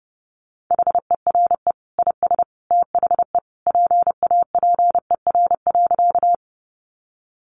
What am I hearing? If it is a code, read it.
Morse code, heard as a keyed tone, about 30 wpm, 700 Hz. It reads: HERE IS THE PAPER.
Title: Sentences from Top 300 Words 30wpm